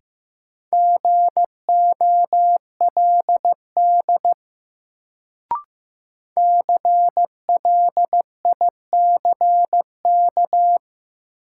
Morse code, heard as a keyed tone, about 15 wpm, 700 Hz. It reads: GOLD CLICK
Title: Sentences from Top 1100 Words - New Word Drill Alt - 15wpm